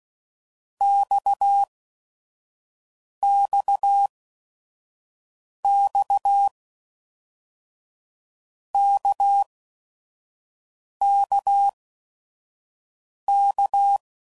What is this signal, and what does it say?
Morse code, heard as a keyed tone, about 16 wpm, spaced as for 5 wpm, 800 Hz. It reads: XXX KKK
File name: Basic Letter Groups By7Groups / RKPX / RKPX@5 f15